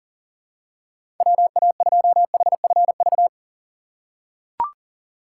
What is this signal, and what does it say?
Morse code, heard as a keyed tone, about 40 wpm, 700 Hz. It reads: WA2HFV
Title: USA Call Signs - 6 Characters - 3x Character Spacing - 40wpm